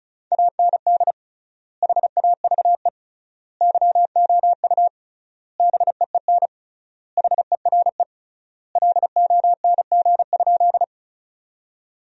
Morse code, heard as a keyed tone, about 34 wpm, 700 Hz. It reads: AND HAVE YOU BEEN HERE LONG?
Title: 3x Word-Spacing - Sentences from Top 200 Words - 35wpm